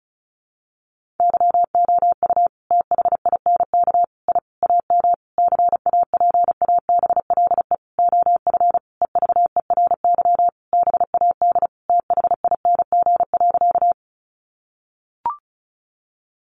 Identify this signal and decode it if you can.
Morse code, heard as a keyed tone, about 34 wpm, 700 Hz. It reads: YOU THINK I AM CAPABLE OF EVERY BAD THING.
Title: Sentences from Top 1200 Words 35wpm